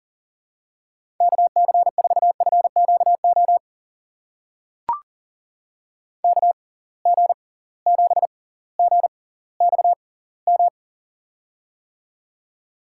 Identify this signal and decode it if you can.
Morse code, heard as a keyed tone, about 37 wpm, spaced as for 20 wpm, 700 Hz. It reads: KC4FQO KG7GXM
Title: USA Call Signs - 6 Characters - 6x Character Spacing - 40wpm